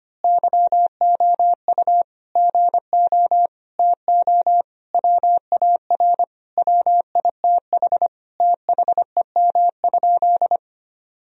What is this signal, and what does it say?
Morse code, heard as a keyed tone, about 25 wpm, 700 Hz. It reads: YOU GO TO WAR WITH THEM?
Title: Sentences from Top 1000 Words - Review Part 2 - 25wpm